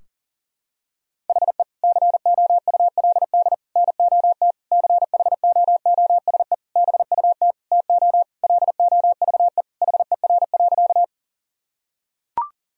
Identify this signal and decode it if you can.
Morse code, heard as a keyed tone, about 40 wpm, 700 Hz. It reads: HE COULD NOT CHOOSE BUT TO LOVE HER.
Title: Sentences from Top 1000 Words - Review Part 4 - 40wpm